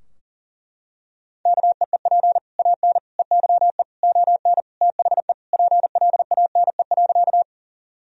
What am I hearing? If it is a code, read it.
Morse code, heard as a keyed tone, about 40 wpm, 700 Hz. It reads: KEEP AN EYE ON THE PLANE.
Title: Sentences from Top 1000 Words - Review Part 3 - 40wpm